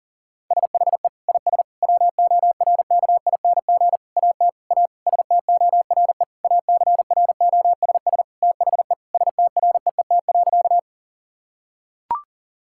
Morse code, heard as a keyed tone, about 40 wpm, 700 Hz. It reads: SHE IS WORKING AT A STORE ACROSS THE STREET.
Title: Sentences from Top 1000 Words - Review Part 5 - 40wpm